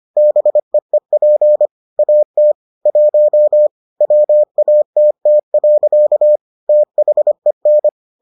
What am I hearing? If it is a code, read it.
Morse code, heard as a keyed tone, about 25 wpm, 600 Hz. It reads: BEEP AT 1 WATT. THEN